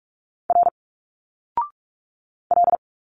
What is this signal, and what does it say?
Morse code, heard as a keyed tone, about 46 wpm, 700 Hz. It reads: R L
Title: Single Letters 45wpm